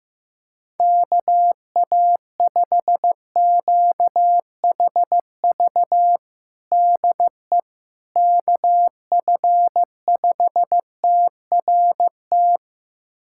Morse code, heard as a keyed tone, about 15 wpm, 700 Hz. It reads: KA5QHV DE KF5TRT
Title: USA Callsign DE Callsign - 6 Characters - 15wpm